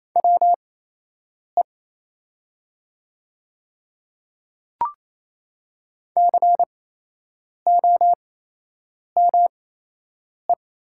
Morse code, heard as a keyed tone, about 28 wpm, spaced as for 8 wpm, 700 Hz. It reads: WE COME
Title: 8x Character Spacing - Top 100 Words - 28wpm